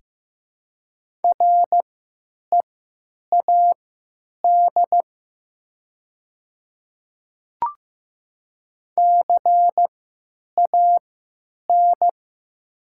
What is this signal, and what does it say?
Morse code, heard as a keyed tone, about 15 wpm, spaced as for 9 wpm, 700 Hz. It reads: READ CAN
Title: 3x Character Spacing - Top 200 Words - 15wpm